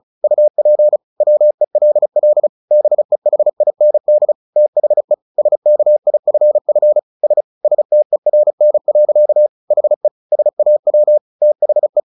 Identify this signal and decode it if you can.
Morse code, heard as a keyed tone, about 34 wpm, 600 Hz. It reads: UP WELL BEHIND THE SKIFF S STERN. HE SAW THE
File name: the old man and the sea 2 35wpm